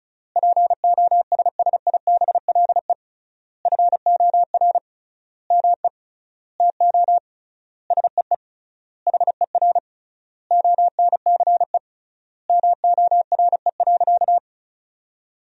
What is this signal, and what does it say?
Morse code, heard as a keyed tone, about 36 wpm, 700 Hz. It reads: POSSIBLE FOR ME TO SEE HER ONCE MORE.
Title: 3x Word-Spacing - Sentences from Top 500 Words - 35wpm